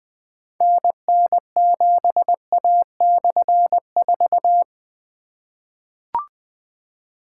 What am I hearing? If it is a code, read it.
Morse code, heard as a keyed tone, about 20 wpm, 700 Hz. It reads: NN7A/4
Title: Difficult Call Signs 20wpm